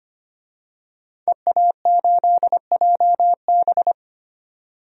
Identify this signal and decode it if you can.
Morse code, heard as a keyed tone, about 25 wpm, 700 Hz. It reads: EA8JB
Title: DXCC-100 Call Signs - 5 Characters - 25wpm